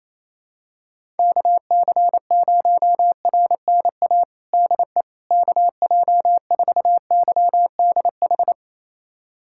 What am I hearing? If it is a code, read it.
Morse code, heard as a keyed tone, about 28 wpm, 700 Hz. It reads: KC0RNA DE KJ4YDH
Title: USA Callsign DE Callsign 28wpm